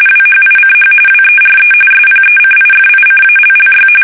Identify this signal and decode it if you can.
RTTY, about 45.45 baud. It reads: WHICH WERE SPECIFICALLY
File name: fsk850